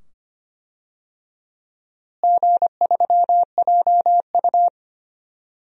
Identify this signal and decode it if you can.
Morse code, heard as a keyed tone, about 25 wpm, 700 Hz. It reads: G3JU
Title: DXCC-100 Call Signs - 4 Characters - Repeated 2x - 25wpm